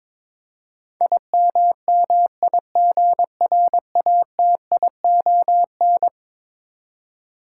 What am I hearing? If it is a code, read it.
Morse code, heard as a keyed tone, about 21 wpm, 700 Hz. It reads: IMMIGRATION
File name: practice / 22wpm / Eleven Letter Words - 22wpm